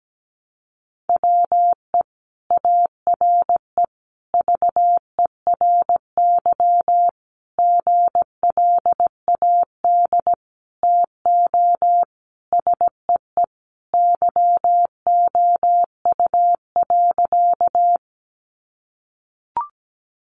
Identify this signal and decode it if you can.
Morse code, heard as a keyed tone, about 17 wpm, 700 Hz. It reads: WE ARE VERY GLAD TO SEE YOU.
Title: Sentences from Top 800 Words 17wpm